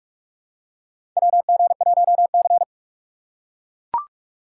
Morse code, heard as a keyed tone, about 44 wpm, 700 Hz. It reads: WG1C
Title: USA Call Signs - 2x Character Spacing - 45wpm